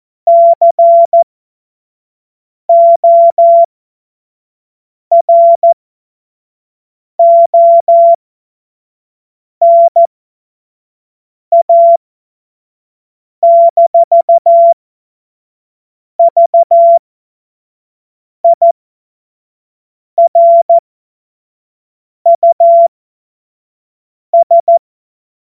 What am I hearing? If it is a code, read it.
Morse code, heard as a keyed tone, about 14 wpm, spaced as for 5 wpm, 680 Hz. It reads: CORONA-VIRUS